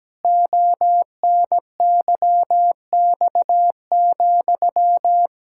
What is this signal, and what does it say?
Morse code, heard as a keyed tone, about 17 wpm, 700 Hz. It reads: ONYX,
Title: Pangram Challenge v2 - 17wpm